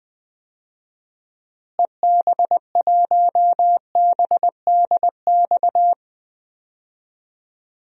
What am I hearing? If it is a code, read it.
Morse code, heard as a keyed tone, about 20 wpm, 700 Hz. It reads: EB1BDX
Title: DXCC-100 Call Signs - 6 Characters - Repeated 2x - 20wpm